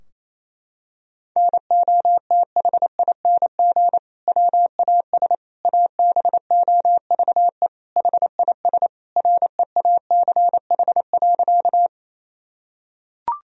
Morse code, heard as a keyed tone, about 28 wpm, 700 Hz. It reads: NOTHING WAS ABOVE HIS REACH.
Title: Sentences from Top 1000 Words - Review Part 2 - 28wpm